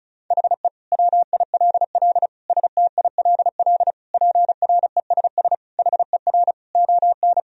Morse code, heard as a keyed tone, about 34 wpm, 690 Hz. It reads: HE WILL STILL PRESS HER ON
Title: Sentences from Top 1000 Words - Review Part 1 - 35wpm